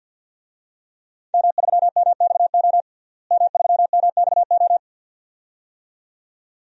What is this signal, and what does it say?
Morse code, heard as a keyed tone, about 49 wpm, 700 Hz. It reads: M3MXO M3MXO
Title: DXCC-100 Call Signs - 5 Characters - Repeated 2x - 50wpm